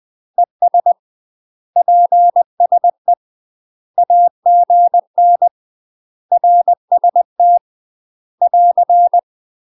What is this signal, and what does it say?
Morse code, heard as a keyed tone, about 20 wpm, 700 Hz. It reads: ES PSE AGN RST <AR>
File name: UK MASTER QSOs TRAINING / 20WPM / UK MASTER QSOs 0049